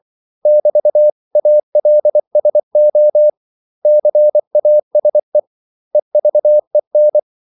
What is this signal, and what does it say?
Morse code, heard as a keyed tone, about 24 wpm, 600 Hz. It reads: <BT>ALSO CASE EVEN
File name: CW / Practice / 500MostCommonEnglishWords24